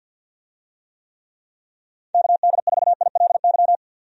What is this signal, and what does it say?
Morse code, heard as a keyed tone, about 51 wpm, 700 Hz. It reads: KD4ILY
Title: USA Call Signs - 6 Characters - 50wpm